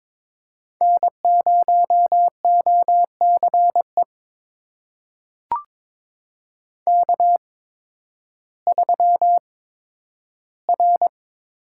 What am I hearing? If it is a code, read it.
Morse code, heard as a keyed tone, about 22 wpm, spaced as for 12 wpm, 700 Hz. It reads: N0OCE K3R
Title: USA Call Signs - 5 Characters - 8x Character Spacing - 22wpm